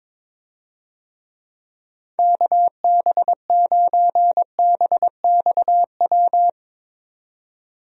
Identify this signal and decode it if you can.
Morse code, heard as a keyed tone, about 22 wpm, 700 Hz. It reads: KB9BXW
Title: USA Call Signs - 6 Characters - 22wpm